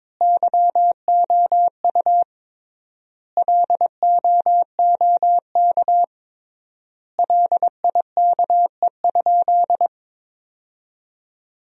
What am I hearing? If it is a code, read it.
Morse code, heard as a keyed tone, about 22 wpm, 700 Hz. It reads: YOU LOOK LIKE?
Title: 3x Word-Spacing - Sentences from Top 100 Words - 22wpm